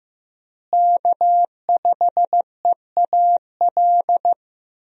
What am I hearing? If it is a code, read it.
Morse code, heard as a keyed tone, about 15 wpm, 700 Hz. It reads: K5EAL
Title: USA Call Signs - 5 Characters - Repeated 3x - 15wpm